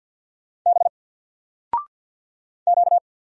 Morse code, heard as a keyed tone, about 49 wpm, 700 Hz. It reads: B Q
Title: Single Letter-Number Alt 50wpm